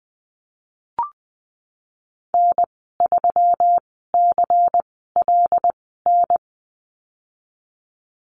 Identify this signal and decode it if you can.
Morse code, heard as a keyed tone, about 20 wpm, spaced as for 14 wpm, 700 Hz. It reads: N3CLN
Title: 122 Lesson 20g - Number 3 - Call Signs 20wpm